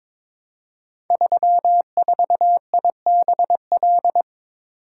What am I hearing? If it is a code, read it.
Morse code, heard as a keyed tone, about 22 wpm, 700 Hz. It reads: 34IBL
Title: Code Groups - 5 Letter:Numbers - 22wpm